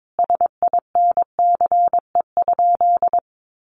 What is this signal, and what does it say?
Morse code, heard as a keyed tone, about 22 wpm, 700 Hz. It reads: SINCE?
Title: Sentences from Top 700 Words 22wpm